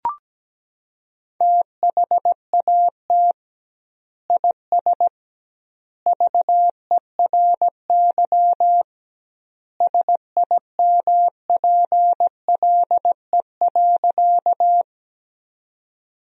Morse code, heard as a keyed tone, about 17 wpm, 700 Hz. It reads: THAT IS VERY SIMPLE.
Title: 2x Word-Spacing - Sentences from Top 400 Words - 17wpm